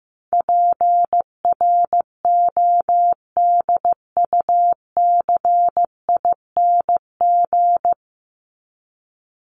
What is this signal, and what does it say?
Morse code, heard as a keyed tone, about 15 wpm, 700 Hz. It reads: PRODUCING
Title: ING Suffix - Top 100 Words with Base Words 15wpm